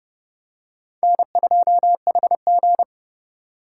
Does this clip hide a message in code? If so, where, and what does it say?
Yes: Morse code from 1.0 to 2.8 s, heard as a keyed tone, about 31 wpm, 700 Hz. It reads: N2HG